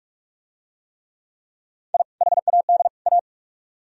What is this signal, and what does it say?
Morse code, heard as a keyed tone, about 40 wpm, 700 Hz. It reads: I HAD A